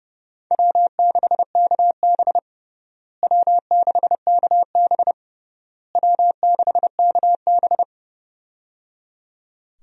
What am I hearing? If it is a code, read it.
Morse code, heard as a keyed tone, about 30 wpm, 700 Hz. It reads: W6KB W6KB W6KB
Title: USA Call Signs - 4 Characters - Repeated 3x - 30wpm